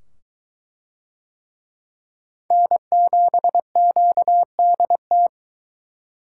Morse code, heard as a keyed tone, about 22 wpm, 700 Hz. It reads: N7QDT